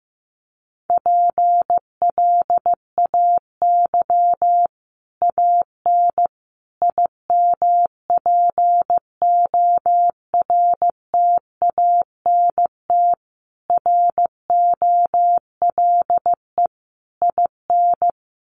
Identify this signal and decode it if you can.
Morse code, heard as a keyed tone, about 14 wpm, 700 Hz. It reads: PLAY AN IMPORTANT ROLE IN
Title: Sets of 5 Words Encore 15wpm